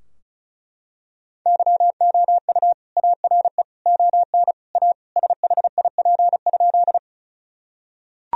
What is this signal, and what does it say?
Morse code, heard as a keyed tone, about 36 wpm, 700 Hz. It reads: YOU ARE ON A SHIP?
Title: Sentences from Top 1000 Words - Review Part 2 - 35wpm